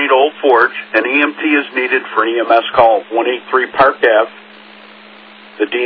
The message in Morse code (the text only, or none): none